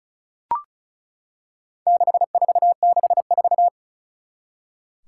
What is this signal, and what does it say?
Morse code, heard as a keyed tone, about 36 wpm, 700 Hz. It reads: 6464
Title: Four Challenging Letter-Number Combinations 35wpm